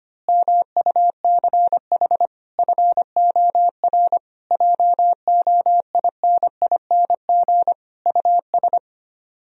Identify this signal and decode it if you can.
Morse code, heard as a keyed tone, about 24 wpm, 700 Hz. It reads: MUCH FOR JOINING US
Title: Speed-Racing - Sets of 5 Words 25wpm